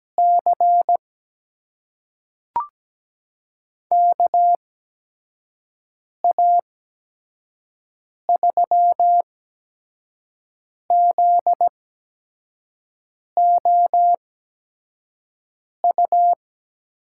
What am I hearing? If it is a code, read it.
Morse code, heard as a keyed tone, about 17 wpm, spaced as for 5 wpm, 700 Hz. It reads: C KA3ZOU
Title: USA Call Signs - 6 Characters - 8x Character Spacing - 17wpm